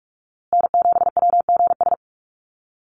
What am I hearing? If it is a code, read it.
Morse code, heard as a keyed tone, about 44 wpm, 700 Hz. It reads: N7WGS
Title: USA Call Signs Repeated 3x 45wpm